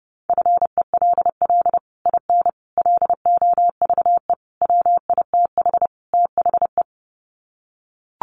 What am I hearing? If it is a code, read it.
Morse code, heard as a keyed tone, about 30 wpm, 700 Hz. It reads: FELL IN LOVE WITH THE
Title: Sets of 5 Words Encore2 30wpm